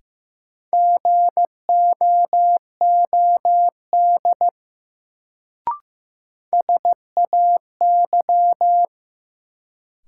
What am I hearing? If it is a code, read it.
Morse code, heard as a keyed tone, about 15 wpm, 700 Hz. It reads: GOOD SAY
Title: Sets of 2 Words - New Word Drill - 15wpm